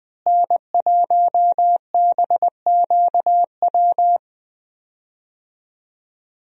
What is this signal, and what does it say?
Morse code, heard as a keyed tone, about 20 wpm, 700 Hz. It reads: N1BQW
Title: USA Call Signs - 5 Characters - Repeated 2x - 20wpm